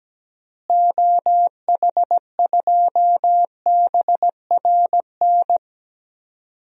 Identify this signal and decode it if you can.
Morse code, heard as a keyed tone, about 17 wpm, 700 Hz. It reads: OH2BRN
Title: DXCC-100 Call Signs - 6 Characters - Repeated 3x - 17wpm